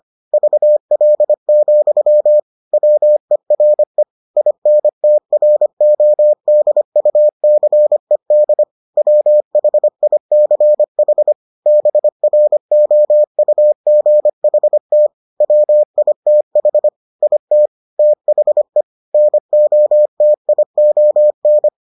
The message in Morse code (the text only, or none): VL, WERE INTRODUCED WHICH BROUGHT WITH IT THE NOTION